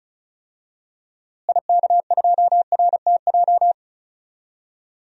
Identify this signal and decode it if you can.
Morse code, heard as a keyed tone, about 36 wpm, 700 Hz. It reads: IK2RTJ
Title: DXCC-100 Call Signs - 6 Characters - Repeated 2x - 35wpm